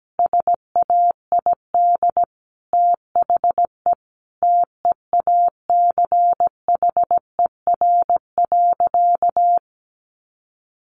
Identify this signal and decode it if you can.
Morse code, heard as a keyed tone, about 17 wpm, 700 Hz. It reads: SAID THE TEACHER.
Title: Sentences from Top 900 Words 17wpm